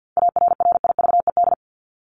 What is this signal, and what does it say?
Morse code, heard as a keyed tone, about 50 wpm, 700 Hz. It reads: ARRIVED
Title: ED Suffix - Top 100 Words with Base Words 50wpm